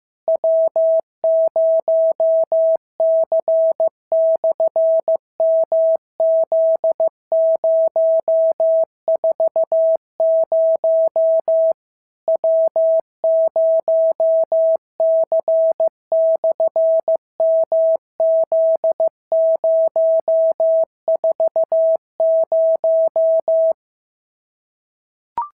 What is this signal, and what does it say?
Morse code, heard as a keyed tone, about 14 wpm, 650 Hz. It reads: W0C/MZ040 W0C/MZ040
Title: SOTA QSO Practice 15wpm